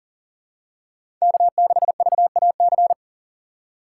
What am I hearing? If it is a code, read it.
Morse code, heard as a keyed tone, about 40 wpm, 700 Hz. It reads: K6VAC
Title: USA Call Signs - 5 Characters - 6x Character Spacing - 40wpm